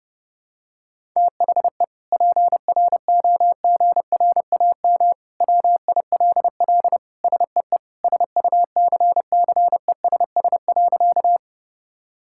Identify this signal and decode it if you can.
Morse code, heard as a keyed tone, about 30 wpm, 700 Hz. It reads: THE PROGRAM WILL SEE SUCCESS.
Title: Sentences from Top 1000 Words - Review Part 5 - 30wpm